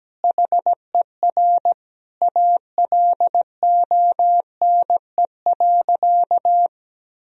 Morse code, heard as a keyed tone, about 17 wpm, 700 Hz. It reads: HER ALONE.
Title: Sentences from Top 1000 Words - Review Part 4 - 17wpm